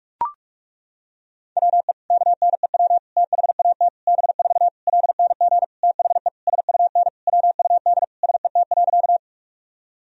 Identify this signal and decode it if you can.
Morse code, heard as a keyed tone, about 42 wpm, 700 Hz. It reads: WE KNEW THAT B4 LNG THE SUN WUD SET.
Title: Sentences from Top 400 Words and Abbreviations 45wpm